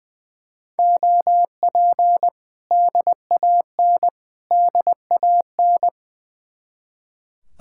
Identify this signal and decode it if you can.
Morse code, heard as a keyed tone, about 20 wpm, 700 Hz. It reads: OP DAN DAN
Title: Speed-Racing Common QSO Phrases Encore - 20wpm